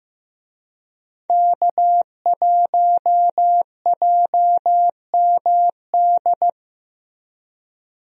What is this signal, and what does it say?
Morse code, heard as a keyed tone, about 15 wpm, 700 Hz. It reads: K1JMD
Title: USA Call Signs - 5 Characters - Repeated 2x - 15wpm